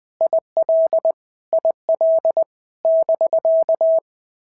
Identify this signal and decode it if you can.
Morse code, heard as a keyed tone, about 20 wpm, 650 Hz. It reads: IL IL <BK>
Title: POTA QSO Practice - 20wpm